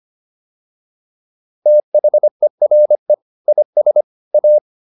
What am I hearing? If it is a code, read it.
Morse code, heard as a keyed tone, about 25 wpm, 600 Hz. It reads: THERE IS A